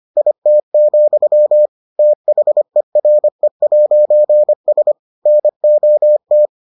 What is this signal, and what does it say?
Morse code, heard as a keyed tone, about 25 wpm, 600 Hz. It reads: IT, THERE'S NOT